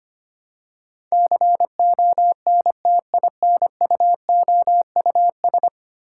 Morse code, heard as a keyed tone, about 25 wpm, 700 Hz. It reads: CONTINUOUS
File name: 10 Letter Words - First 3 Letters Given - 25wpm